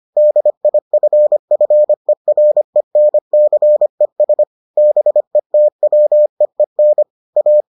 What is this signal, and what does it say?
Morse code, heard as a keyed tone, about 25 wpm, 600 Hz. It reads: DIFFERENCES BETWEEN A